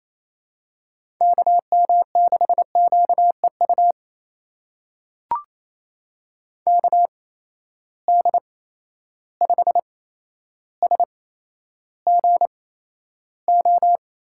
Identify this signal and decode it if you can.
Morse code, heard as a keyed tone, about 28 wpm, spaced as for 12 wpm, 700 Hz. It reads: KM6QEU KD5SGO
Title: USA Call Signs - 6 Characters - 8x Character Spacing - 28wpm